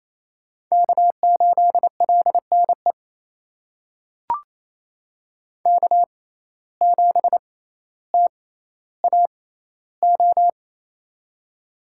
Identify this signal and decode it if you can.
Morse code, heard as a keyed tone, about 28 wpm, spaced as for 14 wpm, 700 Hz. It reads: K8LNE K7TAO
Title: USA Call Signs - 5 Characters - 6x Character Spacing - 28wpm